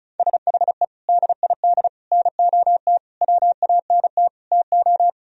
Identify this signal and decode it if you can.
Morse code, heard as a keyed tone, about 37 wpm, 700 Hz. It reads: SHE DID NOT WANT TO
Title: Speed-Racing - Sets of 5 Words 30wpm